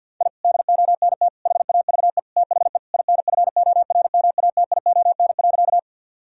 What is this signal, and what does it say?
Morse code, heard as a keyed tone, about 50 wpm, 700 Hz. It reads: I DONT HAVE THE INFORMATION.
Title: Sentences from Top 1000 Words - Review Part 3 - 50wpm